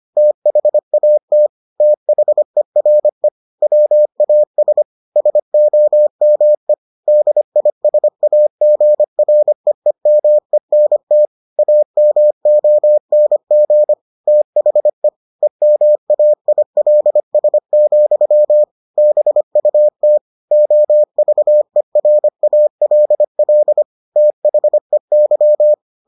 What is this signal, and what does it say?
Morse code, heard as a keyed tone, about 25 wpm, 600 Hz. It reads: THAT THERE WAS SOME DISAGREEMENT AMONG THE EMAILS, BUT OVERALL THEY